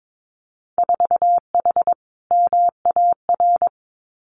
Morse code, heard as a keyed tone, about 22 wpm, 700 Hz. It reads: 4H MAR
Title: Winter Field Day Exchanges Encore 22wpm